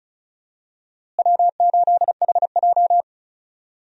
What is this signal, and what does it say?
Morse code, heard as a keyed tone, about 36 wpm, 700 Hz. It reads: W8HJ
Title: USA Call Signs - 4 Characters - Repeated 2x - 35wpm